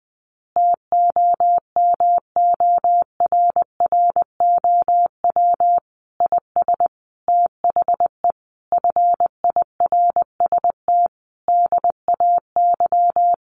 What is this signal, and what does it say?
Morse code, heard as a keyed tone, about 20 wpm, 700 Hz. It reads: TOMORROW IS THE FIRST DAY
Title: Additional Easy Phrases 20wpm